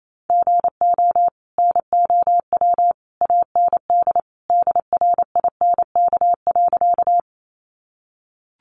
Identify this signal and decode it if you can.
Morse code, heard as a keyed tone, about 28 wpm, 700 Hz. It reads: GO NOW AND DRINK.
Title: Sentences from Top 1500 Words 28wpm